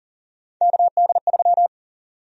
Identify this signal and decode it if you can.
Morse code, heard as a keyed tone, about 39 wpm, 700 Hz. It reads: KD3
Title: USA Call Sign Prefixes - 40wpm